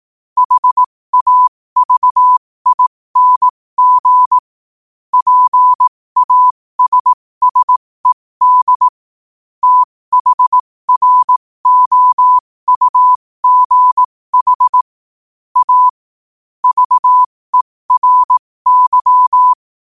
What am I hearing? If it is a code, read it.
Morse code, heard as a keyed tone, about 18 wpm, spaced as for 15 wpm, 1000 Hz. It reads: HAVING PASSED THROUGH A VERY
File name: qotdfiles / 15wpm